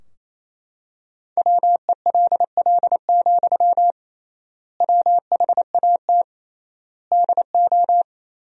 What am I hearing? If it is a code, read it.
Morse code, heard as a keyed tone, about 28 wpm, 700 Hz. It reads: WELL, WHAT DO
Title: 3x Word-Spacing - Sentences from Top 200 Words - 28wpm